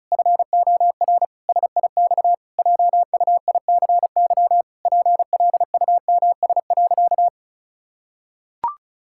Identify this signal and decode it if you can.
Morse code, heard as a keyed tone, about 34 wpm, 700 Hz. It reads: FOR SIX JUICY PLUMS.
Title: Pangram Challenge v2 - 35wpm